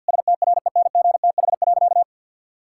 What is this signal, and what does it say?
Morse code, heard as a keyed tone, about 50 wpm, 700 Hz. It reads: STRENGTH.
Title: Sentences from Top 1600 Words 50wpm